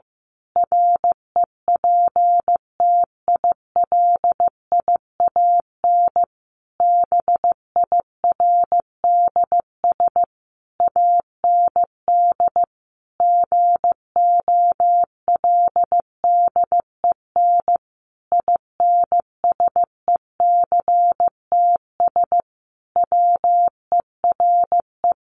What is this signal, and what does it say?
Morse code, heard as a keyed tone, about 15 wpm, 700 Hz. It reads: REPTILIAN BIRDS AND GOLDEN INSECTS WERE
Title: A Sound Of Thunder 15wpm